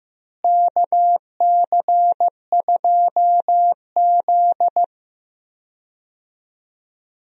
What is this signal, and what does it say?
Morse code, heard as a keyed tone, about 15 wpm, 700 Hz. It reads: KC2Z
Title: USA Call Signs - 4 Characters - Repeated 2x - 15wpm